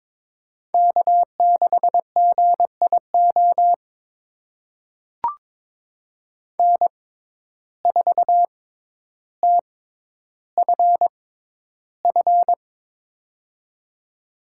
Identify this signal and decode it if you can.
Morse code, heard as a keyed tone, about 22 wpm, spaced as for 11 wpm, 700 Hz. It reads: K6GIO N4TFF
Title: USA Call Signs - 5 Characters - 6x Character Spacing - 22wpm